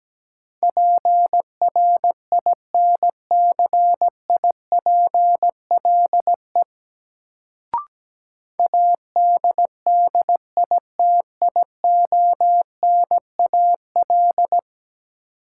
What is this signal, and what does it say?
Morse code, heard as a keyed tone, about 17 wpm, 700 Hz. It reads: PRINCIPLE ADDITIONAL
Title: Sentences from Top 1400 Words - New Word Drill Alt - 17wpm